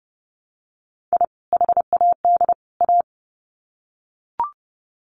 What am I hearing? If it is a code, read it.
Morse code, heard as a keyed tone, about 30 wpm, 700 Hz. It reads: I HAD A
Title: Sets of 3 Words 30wpm